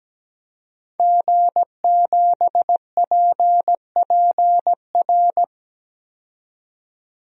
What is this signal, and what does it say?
Morse code, heard as a keyed tone, about 17 wpm, 700 Hz. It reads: G7PPR
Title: DXCC-100 Call Signs - 5 Characters - 17wpm